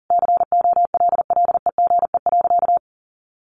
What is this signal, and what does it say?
Morse code, heard as a keyed tone, about 40 wpm, 700 Hz. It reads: COLLEGE.
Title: Sentences from Top 1300 Words 40wpm